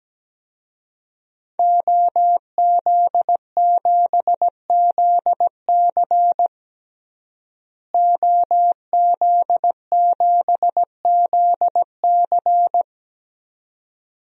Morse code, heard as a keyed tone, about 17 wpm, 700 Hz. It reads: OZ7ZC OZ7ZC
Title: DXCC-100 Call Signs - 5 Characters - Repeated 2x - 17wpm